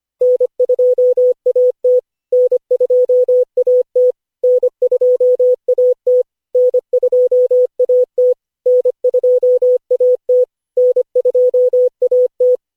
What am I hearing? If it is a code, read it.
Morse code, heard as a keyed tone, about 25 wpm, 500 Hz. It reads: N2AT N2AT N2AT N2AT N2AT N2AT
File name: cw / Session5 cw training